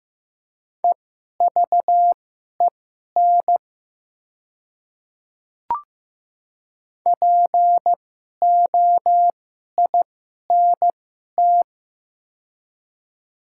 Morse code, heard as a keyed tone, about 15 wpm, spaced as for 11 wpm, 700 Hz. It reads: EVEN POINT
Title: 2x Character Spacing - Top 300 Words - 15wpm